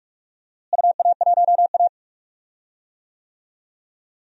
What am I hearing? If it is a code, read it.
Morse code, heard as a keyed tone, about 46 wpm, 700 Hz. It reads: UA1A
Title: DXCC-100 Call Signs - 4 Characters - Repeated 3x - 45wpm